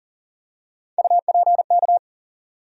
Morse code, heard as a keyed tone, about 40 wpm, 700 Hz. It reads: UPK
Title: USA Call Sign Postfixes - 3 Letters 40wpm